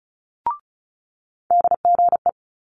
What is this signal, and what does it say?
Morse code, heard as a keyed tone, about 36 wpm, 700 Hz. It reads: DGE
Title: Multi-letter Phonemes Quick Fire 35wpm